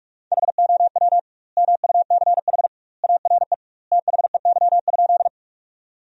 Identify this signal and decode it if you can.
Morse code, heard as a keyed tone, about 46 wpm, 700 Hz. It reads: HOW MUCH ARE THEY?
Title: 2x Word-Spacing - Sentences from Top 200 Words - 45wpm